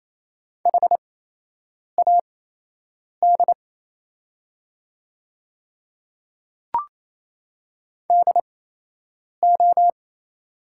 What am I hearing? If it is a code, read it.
Morse code, heard as a keyed tone, about 28 wpm, spaced as for 8 wpm, 700 Hz. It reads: HAD DO
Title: 8x Character Spacing - Top 100 Words - 28wpm